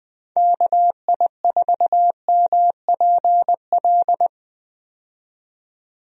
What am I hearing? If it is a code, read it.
Morse code, heard as a keyed tone, about 20 wpm, 700 Hz. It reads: KI4MPL
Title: USA Call Signs - 6 Characters - 20wpm